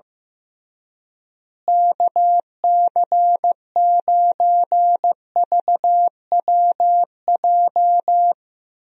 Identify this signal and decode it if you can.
Morse code, heard as a keyed tone, about 15 wpm, 700 Hz. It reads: KC9VWJ
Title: USA Call Signs - 6 Characters - Repeated 3x - 15wpm